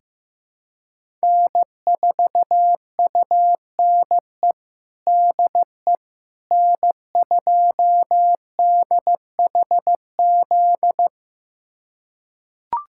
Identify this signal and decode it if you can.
Morse code, heard as a keyed tone, about 15 wpm, 700 Hz. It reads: N4UNE DE N2DHZ
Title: USA Callsign DE Callsign - 5 Characters - 15wpm